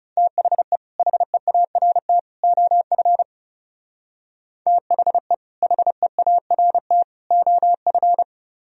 Morse code, heard as a keyed tone, about 30 wpm, 700 Hz. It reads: THE HEART OF THE HEART OF
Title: Speed-Racing - Sets of 3 Words 30wpm